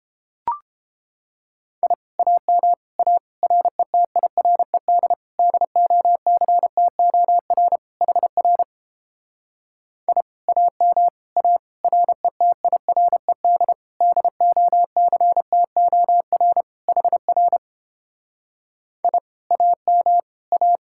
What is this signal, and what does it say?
Morse code, heard as a keyed tone, about 29 wpm, 700 Hz. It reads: I AM A RETIRED DOCTOR HR I AM A RETIRED DOCTOR HR I AM A